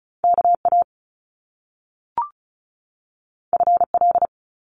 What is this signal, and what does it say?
Morse code, heard as a keyed tone, about 36 wpm, 700 Hz. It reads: KA FL
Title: Top Two Letter Combinations in Words 35wpm